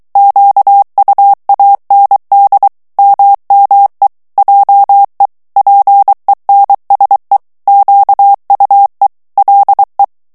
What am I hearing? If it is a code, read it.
Morse code, heard as a keyed tone, about 23 wpm, 800 Hz. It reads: QUAND MME JE PENSE QUE LE
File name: morse mp3 / FRANCAIS TEXTE CLAIR 1 VITESSE 22 MOTS MINUTES